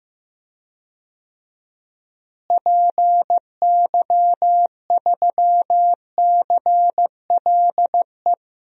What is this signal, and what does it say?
Morse code, heard as a keyed tone, about 16 wpm, 700 Hz. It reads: PY3CLE
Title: DXCC-100 Call Signs - 6 Characters - Repeated 2x - 15wpm